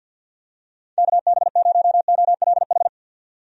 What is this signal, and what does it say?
Morse code, heard as a keyed tone, about 51 wpm, 700 Hz. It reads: KB0OLH
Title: USA Call Signs - 6 Characters - 50wpm